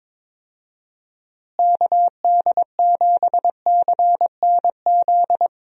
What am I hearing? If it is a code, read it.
Morse code, heard as a keyed tone, about 22 wpm, 700 Hz. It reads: KD7CNZ
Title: USA Call Signs - 6 Characters - Repeated 2x - 22wpm